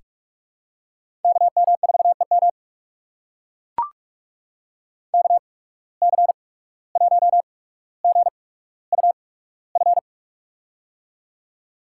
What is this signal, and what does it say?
Morse code, heard as a keyed tone, about 44 wpm, spaced as for 18 wpm, 700 Hz. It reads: KM4EM KC1GUF
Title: USA Call Signs - 8x Character Spacing - 45wpm